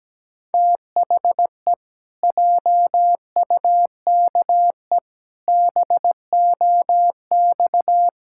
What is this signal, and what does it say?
Morse code, heard as a keyed tone, about 17 wpm, 700 Hz. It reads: THE JUKE BOX
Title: Pangram Challenge v2 - 17wpm